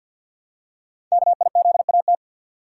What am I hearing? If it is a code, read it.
Morse code, heard as a keyed tone, about 52 wpm, 700 Hz. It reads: KIZAT
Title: Code Groups - 5 Letters - 50wpm